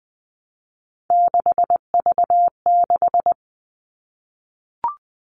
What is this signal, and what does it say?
Morse code, heard as a keyed tone, about 20 wpm, 700 Hz. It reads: 6V6
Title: All Cut Numbers - 3-Digit 20wpm